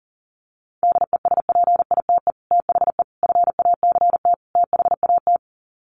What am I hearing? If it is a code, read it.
Morse code, heard as a keyed tone, about 40 wpm, 700 Hz. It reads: DESPITE THE FACT THAT
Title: Sets of 4 Words 40wpm